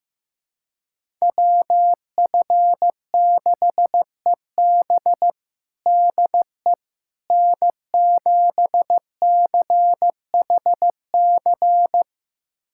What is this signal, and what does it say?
Morse code, heard as a keyed tone, about 15 wpm, 700 Hz. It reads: WF6EB DE N7CHC
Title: USA Callsign DE Callsign - 5 Characters - 15wpm